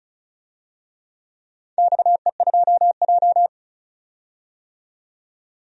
Morse code, heard as a keyed tone, about 36 wpm, 690 Hz. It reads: XE2J
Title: DXCC-100 Call Signs - 4 Characters - 35wpm